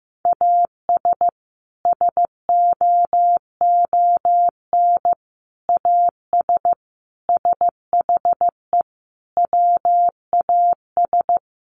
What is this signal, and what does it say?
Morse code, heard as a keyed tone, about 14 wpm, 700 Hz. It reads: AS SOON AS SHE WAS
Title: Sentences from Top 1100 Words 15wpm